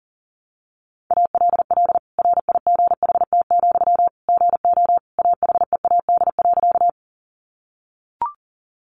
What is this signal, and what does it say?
Morse code, heard as a keyed tone, about 40 wpm, 700 Hz. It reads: ALL RIGHT, GO AHEAD.
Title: Sentences from Top 1100 Words 40wpm